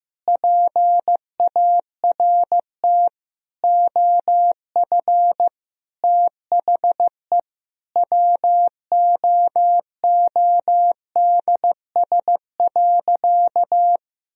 Morse code, heard as a keyed tone, about 15 wpm, 700 Hz. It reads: PART OF THE WOODS.
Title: Sentences from Top 1000 Words - Review Part 2 - 15wpm